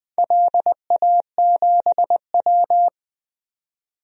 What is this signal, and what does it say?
Morse code, heard as a keyed tone, about 20 wpm, 700 Hz. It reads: LA7W
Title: DXCC-100 Call Signs - 4 Characters - 20wpm